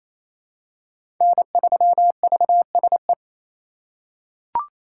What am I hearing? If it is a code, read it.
Morse code, heard as a keyed tone, about 28 wpm, 700 Hz. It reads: N3VSE
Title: USA Call Signs - 5 Characters - 2x Character Spacing - 28wpm